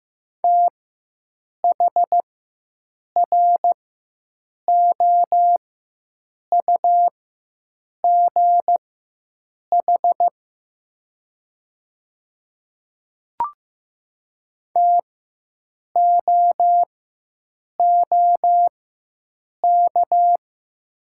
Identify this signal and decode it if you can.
Morse code, heard as a keyed tone, about 15 wpm, spaced as for 7 wpm, 700 Hz. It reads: THROUGH TOOK
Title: 4x Character Spacing - Top 300 Words - 15wpm